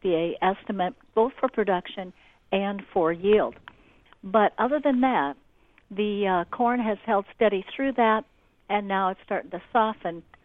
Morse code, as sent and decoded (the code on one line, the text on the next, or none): none